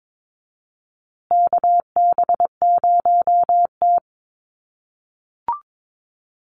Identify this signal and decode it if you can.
Morse code, heard as a keyed tone, about 22 wpm, 700 Hz. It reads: KB0T
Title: USA Call Signs Repeated 3x 22wpm